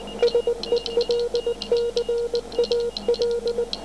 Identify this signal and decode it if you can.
Morse code, heard as a keyed tone, about 19 wpm, 480 Hz. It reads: SELCAL